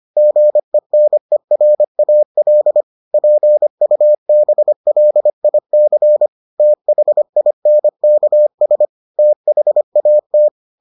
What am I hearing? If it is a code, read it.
Morse code, heard as a keyed tone, about 25 wpm, 600 Hz. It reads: GENERAL PUBLIC THINKS THAT